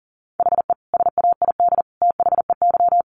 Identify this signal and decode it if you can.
Morse code, heard as a keyed tone, about 40 wpm, 700 Hz. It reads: HE SAID THEY
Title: Sentences from Top 800 Words 40wpm